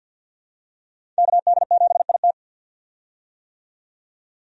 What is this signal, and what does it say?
Morse code, heard as a keyed tone, about 52 wpm, 700 Hz. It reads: KD7IT
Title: USA Call Signs - 5 Characters - 50wpm